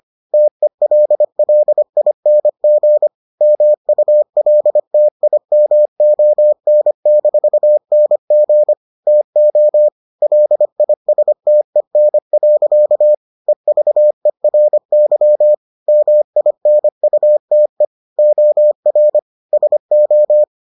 Morse code, heard as a keyed tone, about 25 wpm, 600 Hz. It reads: TELLING MULTIMON-NG TO LISTEN. EVERY MINUTE OR SO